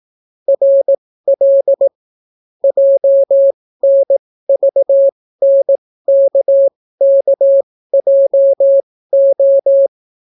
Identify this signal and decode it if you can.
Morse code, heard as a keyed tone, about 18 wpm, spaced as for 14 wpm, 550 Hz. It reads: RL JNVNKKJO